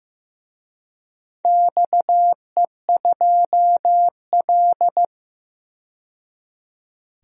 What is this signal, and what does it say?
Morse code, heard as a keyed tone, about 15 wpm, 700 Hz. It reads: XE2L
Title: DXCC-100 Call Signs - 4 Characters - 15wpm